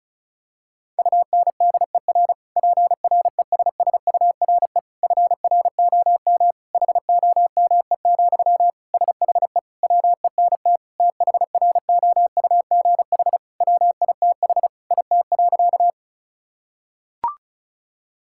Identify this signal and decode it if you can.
Morse code, heard as a keyed tone, about 36 wpm, 700 Hz. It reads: UNDER PRESSURE FROM HOME, SHE WENT THROUGH WITH IT.
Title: Sentences from Top 1000 Words - Review Part 6 - 35wpm